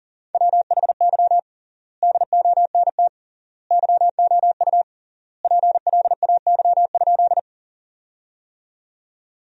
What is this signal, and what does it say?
Morse code, heard as a keyed tone, about 40 wpm, 700 Hz. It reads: WHY DONT YOU PLAY?
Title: 3x Word-Spacing - Sentences from Top 300 Words - 40wpm